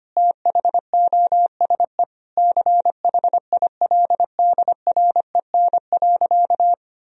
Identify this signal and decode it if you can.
Morse code, heard as a keyed tone, about 25 wpm, 700 Hz. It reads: THOSE CHILDREN.
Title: Sentences from Top 1000 Words - Review Part 1 - 25wpm